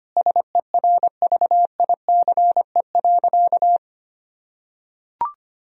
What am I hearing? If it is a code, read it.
Morse code, heard as a keyed tone, about 25 wpm, 700 Hz. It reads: SERVICE.
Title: Sentences from Top 1000 Words - Review Part 4 - 25wpm